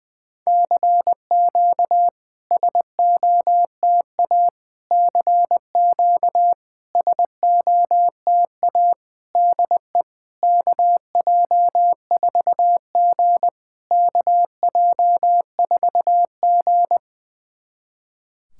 Morse code, heard as a keyed tone, about 20 wpm, 690 Hz. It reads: CQ SOTA CQ SOTA DE KJ4G KJ4G
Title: SOTA QSO Practice 20wpm